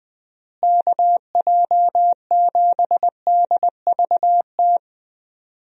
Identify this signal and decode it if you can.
Morse code, heard as a keyed tone, about 21 wpm, 700 Hz. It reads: KJ7DVT
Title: USA Call Signs - 6 Characters - Repeated 3x - 20wpm